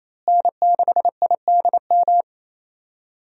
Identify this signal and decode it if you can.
Morse code, heard as a keyed tone, about 28 wpm, 700 Hz. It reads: N6IDM
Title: USA Call Signs - 5 Characters - Repeated 3x - 28wpm